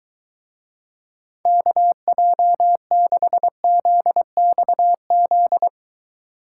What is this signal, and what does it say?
Morse code, heard as a keyed tone, about 23 wpm, 700 Hz. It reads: KJ6ZXZ